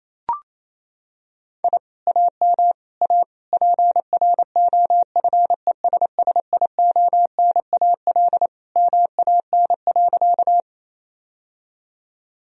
Morse code, heard as a keyed tone, about 28 wpm, 700 Hz. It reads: I AM A PROFESSIONAL MAN.
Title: Sentences from Top 1000 Words - Review Part 4 - 28wpm